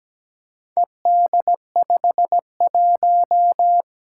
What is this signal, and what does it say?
Morse code, heard as a keyed tone, about 17 wpm, 700 Hz. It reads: ED51
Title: Four Challenging Letter-Number Combinations 17wpm